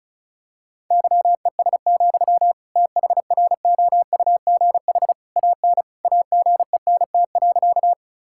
Morse code, heard as a keyed tone, about 36 wpm, 700 Hz. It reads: YES, THROUGH AN AGENT.
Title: Sentences from Top 1600 Words 35wpm